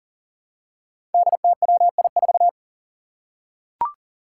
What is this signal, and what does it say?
Morse code, heard as a keyed tone, about 39 wpm, 700 Hz. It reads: DTWI4
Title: Five Challenging Letter-Number Combinations 40wpm